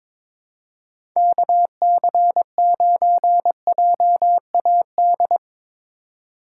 Morse code, heard as a keyed tone, about 22 wpm, 700 Hz. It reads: KC9JAD
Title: USA Call Signs - 6 Characters - 4x Character Spacing - 22wpm